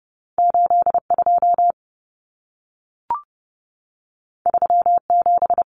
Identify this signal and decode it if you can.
Morse code, heard as a keyed tone, about 30 wpm, 700 Hz. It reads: 82 37
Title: Numbers - 2-Digit 30wpm